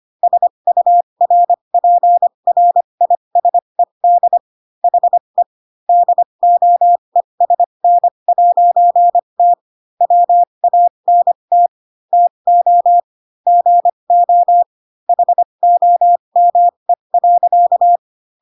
Morse code, heard as a keyed tone, about 25 wpm, spaced as for 22 wpm, 700 Hz. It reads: SURPRISED HE DOESN'T WANT TO GO HOME.
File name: Pinocchio / Pinocchio ch3 18 25 ik1hge